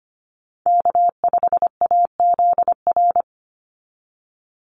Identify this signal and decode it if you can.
Morse code, heard as a keyed tone, about 25 wpm, 700 Hz. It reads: K5AZR
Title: Call Signs - 25wpm